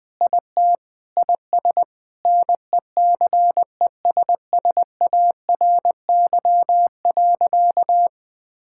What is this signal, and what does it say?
Morse code, heard as a keyed tone, about 20 wpm, 700 Hz. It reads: IT IS NECESSARY.
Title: Sentences with 8-Word Phrases - Spoken Beginning - 20wpm